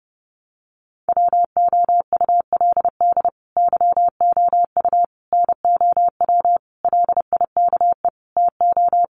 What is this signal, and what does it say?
Morse code, heard as a keyed tone, about 30 wpm, 700 Hz. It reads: WOULD YOU NOW LIKE TO
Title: Sentences from Top 200 Words 30wpm